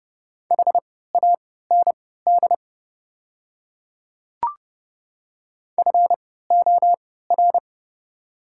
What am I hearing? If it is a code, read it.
Morse code, heard as a keyed tone, about 31 wpm, spaced as for 17 wpm, 690 Hz. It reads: HAND FOR
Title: 3x Character Spacing - Top 200 Words - 30wpm